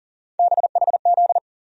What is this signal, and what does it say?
Morse code, heard as a keyed tone, about 40 wpm, 700 Hz. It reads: BHZ
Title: USA Call Sign Postfixes - 3 Letters 40wpm